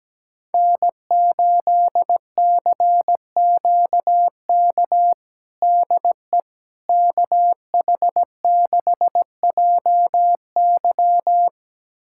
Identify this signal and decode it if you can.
Morse code, heard as a keyed tone, about 17 wpm, 700 Hz. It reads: N8CQK DE KH6JY
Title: USA Callsign DE Callsign - 5 Characters - 17wpm